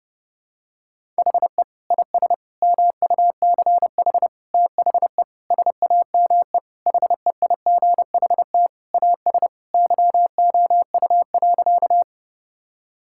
Morse code, heard as a keyed tone, about 32 wpm, 700 Hz. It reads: HE IS MUCH THE SAME HEIGHT AS YOU.
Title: Sentences from Top 1600 Words 30wpm